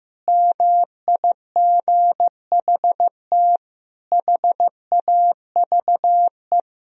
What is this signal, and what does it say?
Morse code, heard as a keyed tone, about 15 wpm, 700 Hz. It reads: MIGHT HAVE
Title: Sentences from Top 1000 Words - Review Part 1 - 15wpm